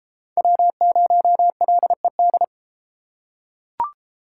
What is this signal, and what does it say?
Morse code, heard as a keyed tone, about 31 wpm, 700 Hz. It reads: W0LED